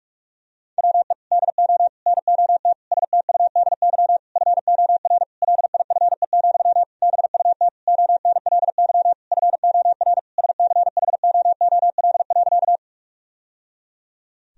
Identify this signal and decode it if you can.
Morse code, heard as a keyed tone, about 46 wpm, 700 Hz. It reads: WE DO NOT STUDY FOR LIFE, BUT ONLY FOR SCHOOL.
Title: Sentences from Top 1000 Words - Review Part 1 - 45wpm